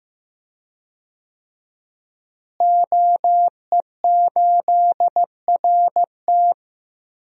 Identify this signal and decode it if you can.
Morse code, heard as a keyed tone, about 15 wpm, 700 Hz. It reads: OE8RT
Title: DXCC-100 Call Signs - 5 Characters - Repeated 3x - 15wpm